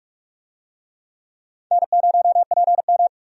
Morse code, heard as a keyed tone, about 47 wpm, 700 Hz. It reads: N0PM
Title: USA Call Signs - 4 Characters - 45wpm